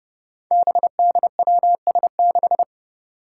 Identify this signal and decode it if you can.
Morse code, heard as a keyed tone, about 30 wpm, 700 Hz. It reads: BDWS6
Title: Five Challenging Letter-Number Combinations 30wpm